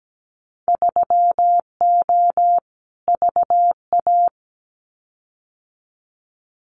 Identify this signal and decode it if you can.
Morse code, heard as a keyed tone, about 17 wpm, 700 Hz. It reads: 3O VA
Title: Winter Field Day Exchanges 17wpm